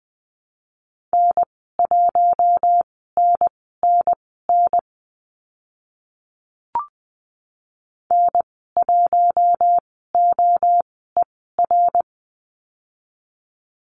Morse code, heard as a keyed tone, about 20 wpm, spaced as for 14 wpm, 700 Hz. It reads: N1NNN N1OER
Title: 082 Lesson 14g - Number 2 - Call Signs 20wpm